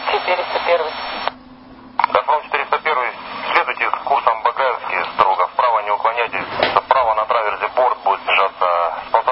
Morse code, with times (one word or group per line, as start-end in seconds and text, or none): none